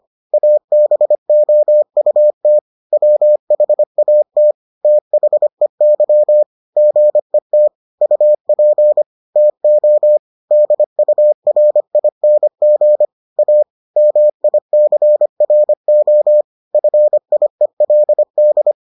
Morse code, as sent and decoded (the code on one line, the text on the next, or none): .- -... --- ..- - / .-- .... .- - / - .... -.-- / --. - / ..- .--. / - --- / -.. ..- .-. .. -. --. / .- / -- .. -.-. .-. --- / ..-. .. .-.. -..
ABOUT WHAT THEY GET UP TO DURING A MICRO FIELD